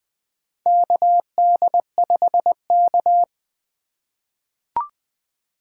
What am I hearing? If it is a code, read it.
Morse code, heard as a keyed tone, about 20 wpm, 700 Hz. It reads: KD5K
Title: USA Call Signs - 4 Characters - 3x Character Spacing - 20wpm